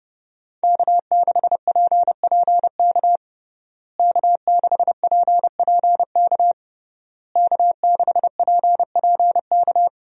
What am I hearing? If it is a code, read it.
Morse code, heard as a keyed tone, about 32 wpm, 700 Hz. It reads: K6PPK K6PPK K6PPK
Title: USA Call Signs - 5 Characters - Repeated 3x - 30wpm